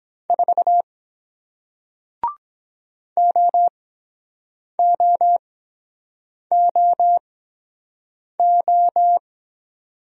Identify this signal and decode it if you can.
Morse code, heard as a keyed tone, about 23 wpm, spaced as for 7 wpm, 700 Hz. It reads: 4 OOOO